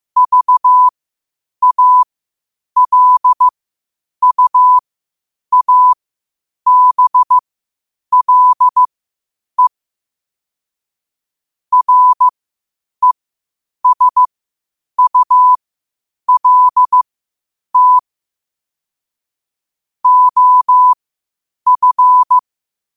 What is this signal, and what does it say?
Morse code, heard as a keyed tone, about 15 wpm, spaced as for 8 wpm, 1000 Hz. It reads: VALUABLE RESULT OF
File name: qotdfiles / 8wpm